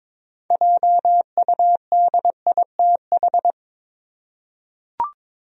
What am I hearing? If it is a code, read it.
Morse code, heard as a keyed tone, about 21 wpm, 700 Hz. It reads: JUDITH
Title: USA Common Names - 22wpm